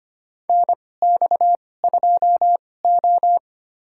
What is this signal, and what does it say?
Morse code, heard as a keyed tone, about 25 wpm, spaced as for 18 wpm, 700 Hz. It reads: NX2O
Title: USA Call Signs - 4 Characters - 2x Character Spacing - 25wpm